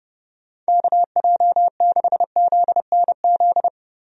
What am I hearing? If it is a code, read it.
Morse code, heard as a keyed tone, about 30 wpm, 700 Hz. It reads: KJ6ZNZ
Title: USA Call Signs - 6 Characters - Repeated 3x - 30wpm